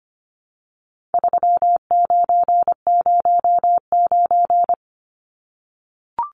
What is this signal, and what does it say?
Morse code, heard as a keyed tone, about 25 wpm, 700 Hz. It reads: 3909
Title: Numbers - 4-Digit 25wpm